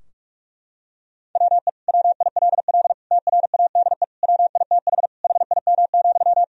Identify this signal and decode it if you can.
Morse code, heard as a keyed tone, about 46 wpm, 700 Hz. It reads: WE WILL TRADE WITH HIM,
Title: Sentences from Top 1000 Words - Review Part 4 - 45wpm